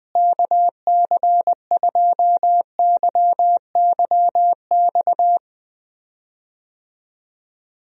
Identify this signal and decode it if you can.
Morse code, heard as a keyed tone, about 20 wpm, 700 Hz. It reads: KC2YYX
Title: USA Call Signs - 6 Characters - Repeated 2x - 20wpm